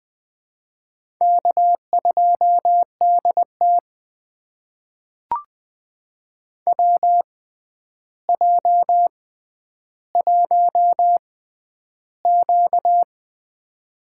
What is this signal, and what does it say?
Morse code, heard as a keyed tone, about 20 wpm, spaced as for 10 wpm, 700 Hz. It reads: K2DT WJ1Q
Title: USA Call Signs - 4 Characters - 6x Character Spacing - 20wpm